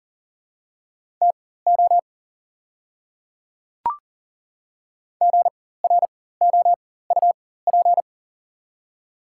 Morse code, heard as a keyed tone, about 40 wpm, spaced as for 19 wpm, 700 Hz. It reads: TO GROUP
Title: 4x Character Spacing - Top 300 Words - 40wpm